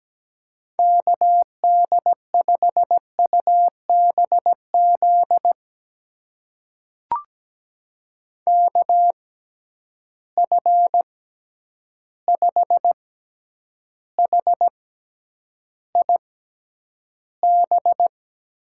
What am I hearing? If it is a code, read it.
Morse code, heard as a keyed tone, about 17 wpm, spaced as for 9 wpm, 700 Hz. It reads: KD5UBZ KF5HIB
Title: USA Call Signs - 6 Characters - 6x Character Spacing - 17wpm